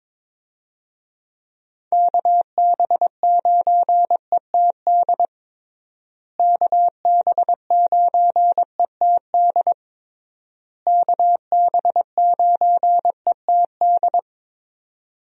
Morse code, heard as a keyed tone, about 22 wpm, 700 Hz. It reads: KB9ETD KB9ETD KB9ETD
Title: USA Call Signs - 6 Characters - Repeated 3x - 22wpm